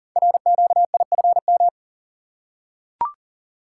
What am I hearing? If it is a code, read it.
Morse code, heard as a keyed tone, about 42 wpm, 700 Hz. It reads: RQIFM
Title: Code Groups - 5 Letters - 40wpm